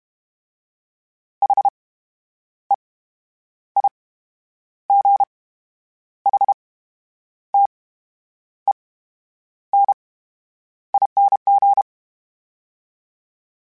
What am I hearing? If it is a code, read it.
Morse code, heard as a keyed tone, about 32 wpm, 800 Hz. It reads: H E I G H T E N ING
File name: Special INGs / Special INGs-mp3s / VerySlowToFaster Ing @30